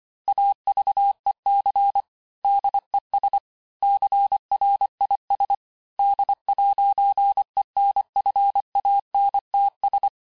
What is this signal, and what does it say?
Morse code, heard as a keyed tone, about 25 wpm, 800 Hz. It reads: AVEC DES CRIS D'ENFANTS
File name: cours-868-teste-clair-vitesse-23mpm